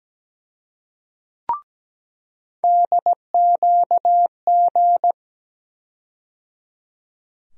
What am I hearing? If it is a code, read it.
Morse code, heard as a keyed tone, about 17 wpm, 700 Hz. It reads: DQG
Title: USA Call Sign Postfixes - 3 Letters 17wpm